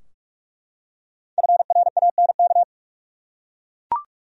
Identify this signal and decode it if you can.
Morse code, heard as a keyed tone, about 41 wpm, 700 Hz. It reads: FRANK
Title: USA Common Names - 45wpm